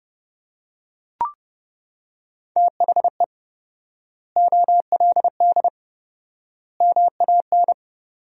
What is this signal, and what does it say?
Morse code, heard as a keyed tone, about 32 wpm, 700 Hz. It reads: THE OLD MAN
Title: 4x Word-Spacing - Sentences from Top 500 Words - 30wpm